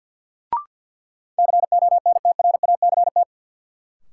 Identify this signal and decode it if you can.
Morse code, heard as a keyed tone, about 52 wpm, 700 Hz. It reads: CONTRACT
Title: Sentences from Top 1400 Words - New Word Drill - 50wpm